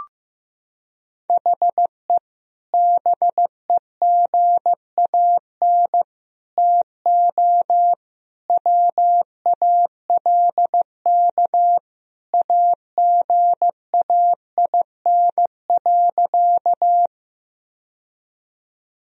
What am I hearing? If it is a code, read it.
Morse code, heard as a keyed tone, about 15 wpm, 700 Hz. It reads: HE BEGAN TO WALK AGAIN.
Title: Sentences from Top 1000 Words - Review Part 1 - 15wpm